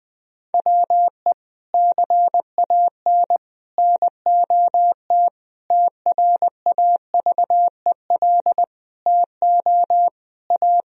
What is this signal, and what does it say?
Morse code, heard as a keyed tone, about 20 wpm, 700 Hz. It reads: WE CAN NOT TRAVEL TO A
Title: Sentences from Top 1000 Words - Review Part 2 - 20wpm